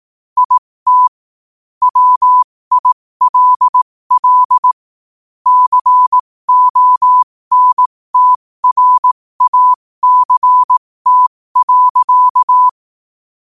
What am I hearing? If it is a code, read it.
Morse code, heard as a keyed tone, about 18 wpm, spaced as for 15 wpm, 1000 Hz. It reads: IT WILL CONTRACT.
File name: qotdfiles / 15wpm